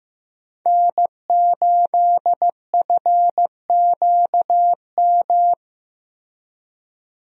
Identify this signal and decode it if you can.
Morse code, heard as a keyed tone, about 15 wpm, 700 Hz. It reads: N8FQM
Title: USA Call Signs - 5 Characters - Repeated 2x - 15wpm